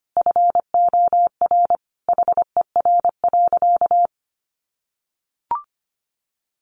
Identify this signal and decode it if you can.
Morse code, heard as a keyed tone, about 25 wpm, 700 Hz. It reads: FOR HER.
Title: Sentences from Top 1400 Words 25wpm